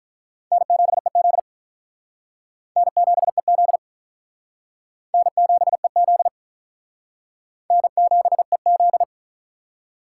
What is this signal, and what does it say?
Morse code, heard as a keyed tone, about 44 wpm, 700 Hz. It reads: N7EZ N7EZ N7EZ N7EZ